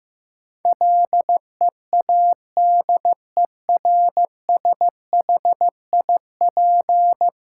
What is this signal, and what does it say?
Morse code, heard as a keyed tone, about 16 wpm, 700 Hz. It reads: LEADERSHIP
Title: Sentences from Top 1600 Words - New Word Drill - 15wpm